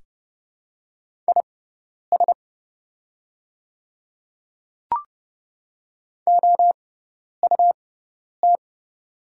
Morse code, heard as a keyed tone, about 30 wpm, spaced as for 10 wpm, 700 Hz. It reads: IS OUT
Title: 6x Character Spacing - Top 100 Words - 30wpm